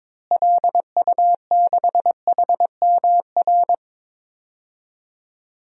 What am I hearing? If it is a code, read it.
Morse code, heard as a keyed tone, about 21 wpm, 700 Hz. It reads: LU6HMR
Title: DXCC-100 Call Signs - 6 Characters - Repeated 3x - 22wpm